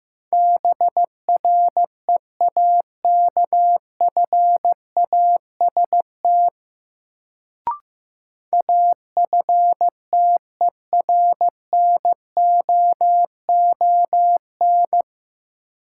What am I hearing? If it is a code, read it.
Morse code, heard as a keyed tone, about 15 wpm, 700 Hz. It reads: BREAKFAST AFTERNOON
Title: Sentences from Top 1500 Words - New Word Drill - 15wpm